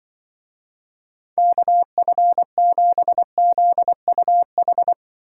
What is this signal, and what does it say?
Morse code, heard as a keyed tone, about 23 wpm, 700 Hz. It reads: KF7ZUH
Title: Speed Racing - USA 2x3 Call Signs 20wpm